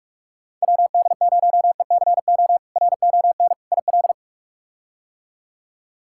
Morse code, heard as a keyed tone, about 44 wpm, 700 Hz. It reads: WD0ECO RON IL
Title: K1USN SST QSO Practice - 45wpm